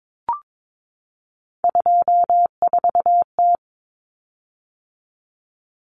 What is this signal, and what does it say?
Morse code, heard as a keyed tone, about 22 wpm, 700 Hz. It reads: 24T
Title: Common Cut Numbers - 3-Digit 22wpm